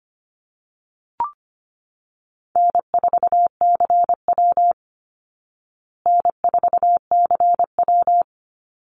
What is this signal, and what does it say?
Morse code, heard as a keyed tone, about 25 wpm, 700 Hz. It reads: N4CW N4CW
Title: Call Signs Repeated 2x - 25wpm